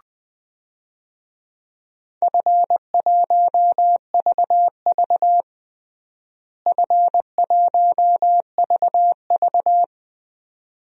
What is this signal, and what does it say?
Morse code, heard as a keyed tone, about 20 wpm, 700 Hz. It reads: F1VV F1VV
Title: DXCC-100 Call Signs - 4 Characters - Repeated 2x - 20wpm